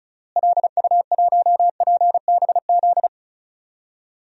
Morse code, heard as a keyed tone, about 34 wpm, 700 Hz. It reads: LU1PBZ
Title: DXCC-100 Call Signs - 6 Characters - Repeated 2x - 35wpm